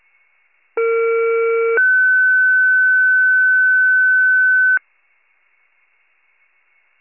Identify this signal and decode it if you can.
Two-tone page, heard as a one-second tone, then a three-second tone, about 460 then 1500 Hz. It is heard from 0.8 to 4.8 s.
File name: pager / 41 FIRE / wcalert